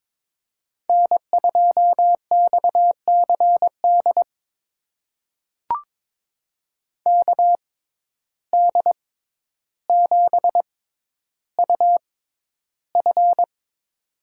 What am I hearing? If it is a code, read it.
Morse code, heard as a keyed tone, about 22 wpm, spaced as for 11 wpm, 700 Hz. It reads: N2XCD KD7UF
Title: USA Call Signs - 5 Characters - 6x Character Spacing - 22wpm